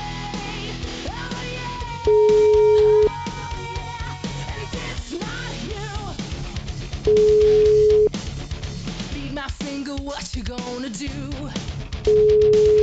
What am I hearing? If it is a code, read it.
Ringback tone (Europe, 425 Hz).